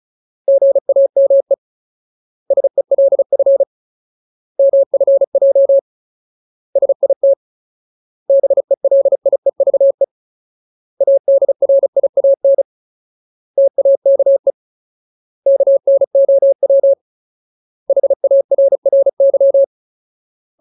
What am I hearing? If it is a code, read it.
Morse code, heard as a keyed tone, about 34 wpm, 550 Hz. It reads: GAME SELF MFJ SIT BELIEVE ADRIAN TAKE KNOW HARRY
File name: Ultimate Word List / 35WPM Ultimate